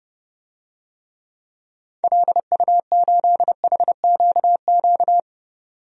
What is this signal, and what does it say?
Morse code, heard as a keyed tone, about 32 wpm, 700 Hz. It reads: LU8HQQ
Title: DXCC-100 Call Signs - 6 Characters - 30wpm